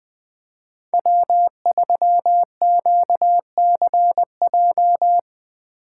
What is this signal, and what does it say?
Morse code, heard as a keyed tone, about 21 wpm, 700 Hz. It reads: W3QCJ
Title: USA Call Signs - 5 Characters - 8x Character Spacing - 20wpm